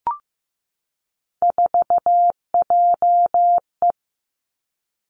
Morse code, heard as a keyed tone, about 15 wpm, 700 Hz. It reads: 4JE
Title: Three Challenging Letter-Number Combinations 15wpm